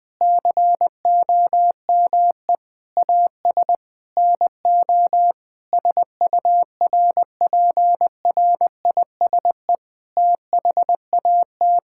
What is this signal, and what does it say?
Morse code, heard as a keyed tone, about 20 wpm, 700 Hz. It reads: COME AS NO SURPRISE THAT
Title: Sets of 8 Words - 20wpm